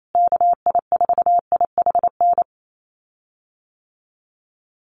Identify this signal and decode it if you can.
Morse code, heard as a keyed tone, about 28 wpm, 700 Hz. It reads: KI4IHN
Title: USA Call Signs Repeated 3x 28wpm